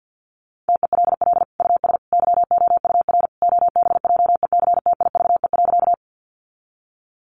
Morse code, heard as a keyed tone, about 50 wpm, 700 Hz. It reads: TELL US YOUR OBJECTIVE.
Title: Sentences from Top 1200 Words 50wpm